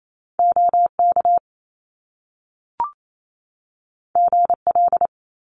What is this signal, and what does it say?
Morse code, heard as a keyed tone, about 28 wpm, 700 Hz. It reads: OK GL
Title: Top Two Letter Combinations in Words 28wpm